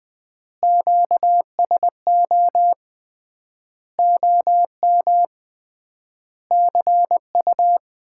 Morse code, heard as a keyed tone, about 21 wpm, 700 Hz. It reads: QSO OM CU
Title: Common QSO Phrases - 3x Word-Spacing - 20wpm